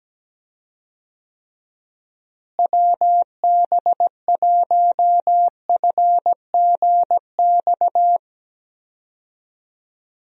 Morse code, heard as a keyed tone, about 17 wpm, 700 Hz. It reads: WB1FGX
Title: USA Call Signs - 6 Characters - Repeated 3x - 17wpm